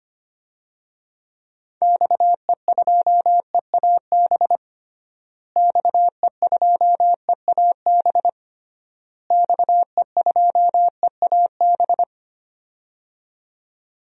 Morse code, heard as a keyed tone, about 25 wpm, 700 Hz. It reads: XE2EAB XE2EAB XE2EAB
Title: DXCC-100 Call Signs - 6 Characters - Repeated 3x - 25wpm